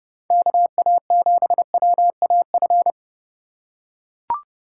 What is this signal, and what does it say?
Morse code, heard as a keyed tone, about 31 wpm, 700 Hz. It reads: KA7WAF